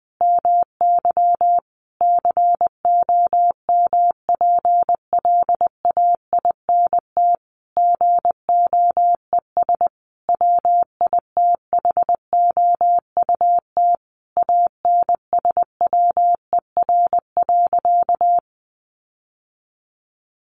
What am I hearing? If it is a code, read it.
Morse code, heard as a keyed tone, about 20 wpm, 700 Hz. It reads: MY COMPLAINT GOES WITHOUT ANSWER.
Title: Sentences from Top 1500 Words 20wpm